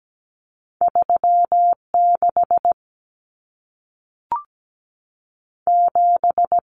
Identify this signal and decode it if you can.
Morse code, heard as a keyed tone, about 17 wpm, 700 Hz. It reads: 36 7
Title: Numbers - 2-Digit 17wpm